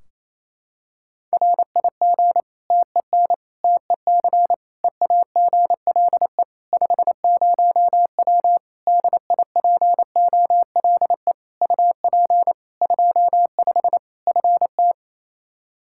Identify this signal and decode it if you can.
Morse code, heard as a keyed tone, about 28 wpm, 700 Hz. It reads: RIG TEN TEC EAGLE 50W DIPOLE UP 25 FT
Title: Practice QSO - Exchange 2 - Info - 28wpm